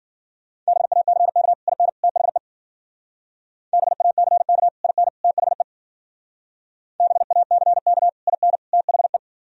Morse code, heard as a keyed tone, about 50 wpm, 700 Hz. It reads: BACK IN THE BACK IN THE BACK IN THE
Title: Speed-Racing - Sets of 3 Words 40wpm